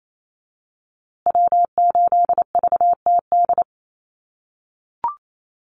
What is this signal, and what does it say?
Morse code, heard as a keyed tone, about 28 wpm, 700 Hz. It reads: W8VTD
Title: USA Call Signs Repeated 3x 28wpm